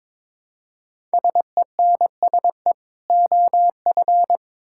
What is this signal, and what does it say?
Morse code, heard as a keyed tone, about 22 wpm, 700 Hz. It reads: SENSE OF
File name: Build A Phrase - 2-Word Phrases - 22wpm